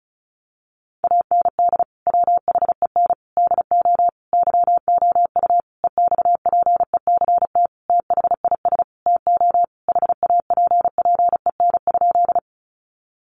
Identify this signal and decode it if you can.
Morse code, heard as a keyed tone, about 34 wpm, 700 Hz. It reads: AND WHEN DO YOU EXPECT THIS TO HAPPEN?
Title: Sentences from Top 1100 Words 35wpm